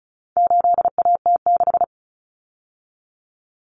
Morse code, heard as a keyed tone, about 34 wpm, 700 Hz. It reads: 8AT6
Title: Common Cut Numbers - 4-Digit 35wpm